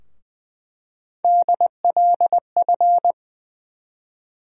Morse code, heard as a keyed tone, about 21 wpm, 700 Hz. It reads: DLF
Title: USA Call Sign Postfixes - 3 Letters 20wpm